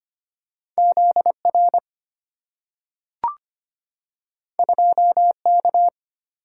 Morse code, heard as a keyed tone, about 25 wpm, 700 Hz. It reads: Z R 2 K
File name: Two Letter-Number Combinations - 25wpm